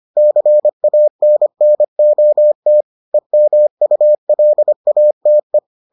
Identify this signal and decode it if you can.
Morse code, heard as a keyed tone, about 25 wpm, 600 Hz. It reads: CANNOT EMULATE